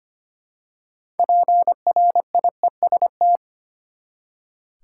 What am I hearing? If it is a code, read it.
Morse code, heard as a keyed tone, about 25 wpm, 700 Hz. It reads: PRIEST
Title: Six Letter Words - 25wpm